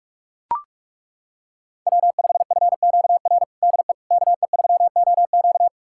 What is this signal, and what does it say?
Morse code, heard as a keyed tone, about 42 wpm, 700 Hz. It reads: W5FQR DE KE3OQ
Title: USA Callsign DE Callsign - 5 Characters - 45wpm